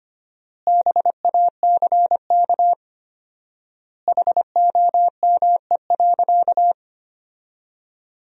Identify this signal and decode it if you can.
Morse code, heard as a keyed tone, about 25 wpm, 700 Hz. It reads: BACK HOME.
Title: 4x Word-Spacing - Sentences from Top 500 Words - 25wpm